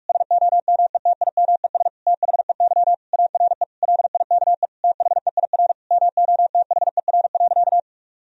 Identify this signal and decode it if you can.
Morse code, heard as a keyed tone, about 42 wpm, 700 Hz. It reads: SOMETIMES THEY ARE LIKE THEIR MOTHER.
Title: Sentences from Top 1000 Words - Review Part 3 - 45wpm